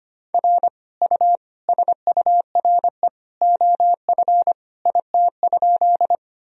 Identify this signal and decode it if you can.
Morse code, heard as a keyed tone, about 25 wpm, 700 Hz. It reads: R U SURE OF IT?
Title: Sentences from Top 300 Words and Abbreviations 25wpm